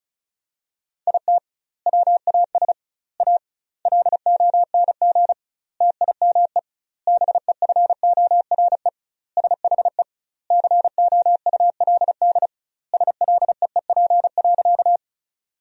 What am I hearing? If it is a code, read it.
Morse code, heard as a keyed tone, about 36 wpm, 700 Hz. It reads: IT WAS A LONG TIME BEFORE SHE COULD SLEEP.
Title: 2x Word-Spacing - Sentences from Top 500 Words - 35wpm